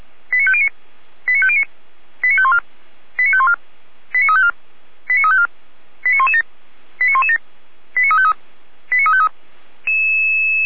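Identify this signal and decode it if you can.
Five-tone selective call (ZVEI), heard as quick runs of five tones, tones of about 70 ms. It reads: 89509 89509 89514 89514 89354 89354 89117 89117 89353 89353
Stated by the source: ZVEI five-tone selective call received on an SDR, fire brigade Orlamünde (feuerwehr-orlamuende).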